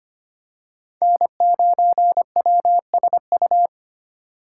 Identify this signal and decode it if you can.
Morse code, heard as a keyed tone, about 25 wpm, 700 Hz. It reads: N9WSU
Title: USA Call Signs - 5 Characters - 6x Character Spacing - 25wpm